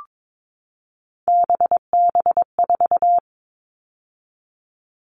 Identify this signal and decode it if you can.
Morse code, heard as a keyed tone, about 22 wpm, 700 Hz. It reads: BB4
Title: All Cut Numbers - 3-Digit 22wpm